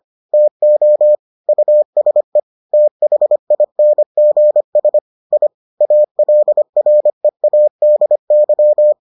Morse code, heard as a keyed tone, about 25 wpm, 600 Hz. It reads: TO USE THINGS I ALREADY